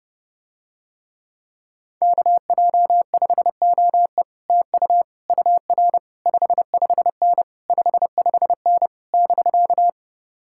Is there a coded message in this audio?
Yes: Morse code from 2.0 to 9.9 s, heard as a keyed tone, about 32 wpm, 700 Hz. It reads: KJ5OE TU UR 55N 55N <BK>